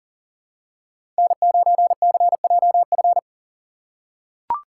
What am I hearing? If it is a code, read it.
Morse code, heard as a keyed tone, about 38 wpm, 700 Hz. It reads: N9CJF
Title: USA Call Signs - 5 Characters - 4x Character Spacing - 40wpm